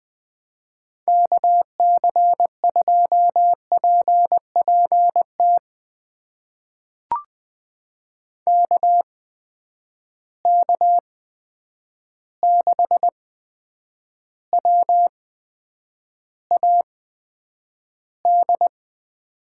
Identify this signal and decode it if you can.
Morse code, heard as a keyed tone, about 20 wpm, spaced as for 9 wpm, 700 Hz. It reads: KC2PPT KK6WAD
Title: USA Call Signs - 6 Characters - 8x Character Spacing - 20wpm